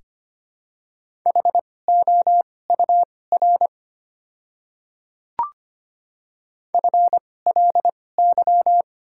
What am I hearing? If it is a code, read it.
Morse code, heard as a keyed tone, about 25 wpm, spaced as for 18 wpm, 700 Hz. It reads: HOUR FLY
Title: 2x Character Spacing - Top 500 Words - 25wpm